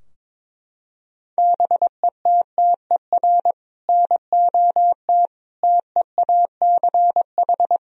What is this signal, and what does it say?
Morse code, heard as a keyed tone, about 22 wpm, 700 Hz. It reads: BETTER NOT TEACH
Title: Sentences from Top 1000 Words - Review Part 2 - 22wpm